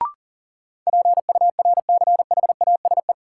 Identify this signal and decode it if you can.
Morse code, heard as a keyed tone, about 40 wpm, 700 Hz. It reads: PURCHASE
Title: Sentences from Top 800 Words - New Word Drill - 40wpm